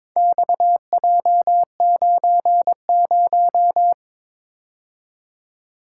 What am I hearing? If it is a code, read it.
Morse code, heard as a keyed tone, about 22 wpm, 700 Hz. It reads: XJ90
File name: practice / 22wpm / Four Letter-Number Combinations - 22wpm